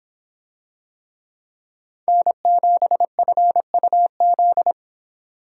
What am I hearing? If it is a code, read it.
Morse code, heard as a keyed tone, about 24 wpm, 700 Hz. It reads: N7FUZ